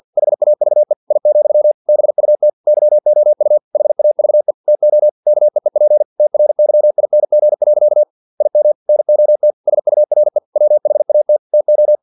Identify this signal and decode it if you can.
Morse code, heard as a keyed tone, about 50 wpm, 600 Hz. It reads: HAVE I, BUT YOU HAVE TO KEEP TRYING. IM NOT SURE WHAT TO
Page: Never 49wpm